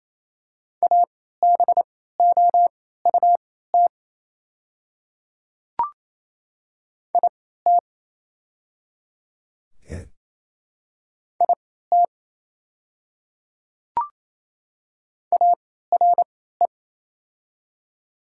Morse code, heard as a keyed tone, about 27 wpm, spaced as for 16 wpm, 700 Hz. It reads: ABOUT IT IT ARE
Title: 3x Character Spacing - Top 200 Words - 28wpm